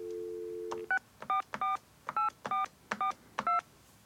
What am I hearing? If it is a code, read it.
DTMF, heard as telephone touch tones, tones of about 140 ms. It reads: 6448442